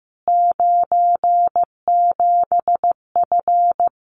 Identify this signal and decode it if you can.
Morse code, heard as a keyed tone, about 15 wpm, 700 Hz. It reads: 97F
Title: Common QSO Phrases Encore 15wpm